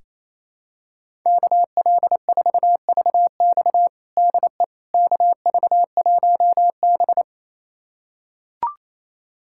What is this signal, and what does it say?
Morse code, heard as a keyed tone, about 27 wpm, 700 Hz. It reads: KL4VX DE KV1B
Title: USA Callsign DE Callsign 28wpm